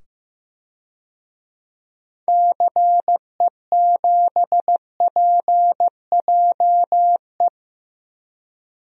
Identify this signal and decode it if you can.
Morse code, heard as a keyed tone, about 15 wpm, 700 Hz. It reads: CE7PJE
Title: DXCC-100 Call Signs - 6 Characters - 15wpm